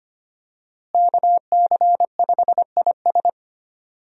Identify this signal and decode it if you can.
Morse code, heard as a keyed tone, about 25 wpm, 700 Hz. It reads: KC5IS
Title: USA Call Signs - 5 Characters - Repeated 3x - 25wpm